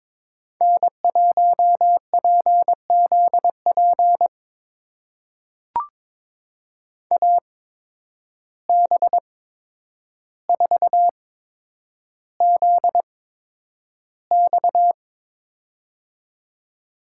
Morse code, heard as a keyed tone, about 22 wpm, spaced as for 9 wpm, 700 Hz. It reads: N1PZP AB4ZX
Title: USA Call Signs - 5 Characters - 8x Character Spacing - 22wpm